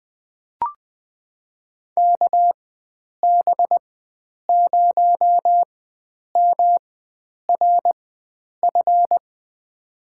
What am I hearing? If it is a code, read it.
Morse code, heard as a keyed tone, about 20 wpm, spaced as for 9 wpm, 700 Hz. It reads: KB0MRF
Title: USA Call Signs - 6 Characters - 4x Character Spacing - 20wpm